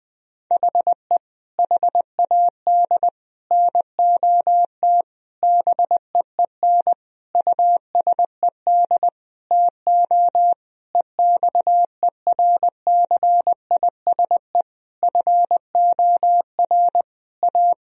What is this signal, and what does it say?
Morse code, heard as a keyed tone, about 20 wpm, 700 Hz. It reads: HE HAD NOT BEEN USED TO EXERCISE FOR A
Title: Sentences from Top 1000 Words - Review Part 4 - 20wpm